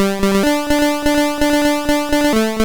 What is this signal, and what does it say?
There is no code.